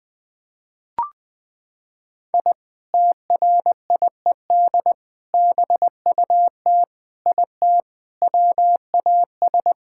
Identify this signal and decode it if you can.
Morse code, heard as a keyed tone, about 21 wpm, 700 Hz. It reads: I TRIED BUT IT WAS
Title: Sentences with 3-Word Phrases - 20wpm